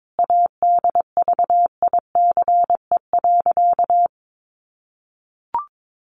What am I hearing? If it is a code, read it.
Morse code, heard as a keyed tone, about 22 wpm, 700 Hz. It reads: ADVICE.
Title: Sentences from Top 1100 Words 22wpm